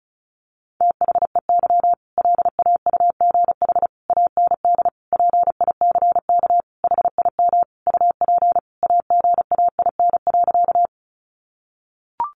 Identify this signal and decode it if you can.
Morse code, heard as a keyed tone, about 34 wpm, 700 Hz. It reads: THEY LAUGH AND PICK HIM UP AGAIN.
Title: Sentences from Top 1000 Words 35wpm